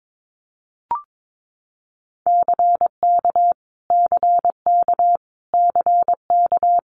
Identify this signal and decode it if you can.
Morse code, heard as a keyed tone, about 22 wpm, 700 Hz. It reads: CK CK CK
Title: Multi-letter Phonemes Repeated 3x 22wpm